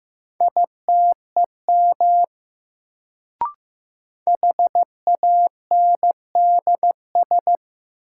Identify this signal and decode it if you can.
Morse code, heard as a keyed tone, about 15 wpm, 700 Hz. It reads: ITEM HANDS
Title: Sentences from Top 1000 Words - Review Part 6 - New Word Drill - 15wpm